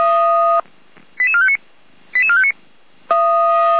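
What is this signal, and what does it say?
Five-tone selective call (ZVEI), heard as quick runs of five tones, tones of about 70 ms. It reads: 88459 88459
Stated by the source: ZVEI five-tone call 88457 received by SDR, Feuerwehr Orlamünde.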